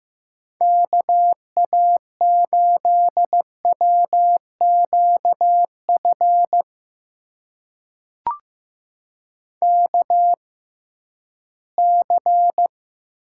Morse code, heard as a keyed tone, about 15 wpm, spaced as for 11 wpm, 700 Hz. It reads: KA8WQF KC
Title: USA Call Signs - 6 Characters - 6x Character Spacing - 15wpm